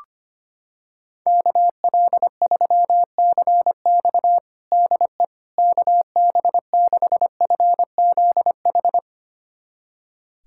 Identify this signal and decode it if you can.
Morse code, heard as a keyed tone, about 25 wpm, 700 Hz. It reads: KL3CX DE KB6FZH
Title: USA Callsign DE Callsign 25wpm